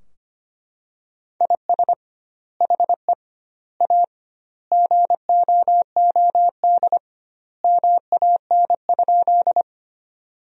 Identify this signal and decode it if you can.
Morse code, heard as a keyed tone, about 25 wpm, 700 Hz. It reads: IS HE A GOOD MAN?
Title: 2x Word-Spacing - Sentences from Top 200 Words - 25wpm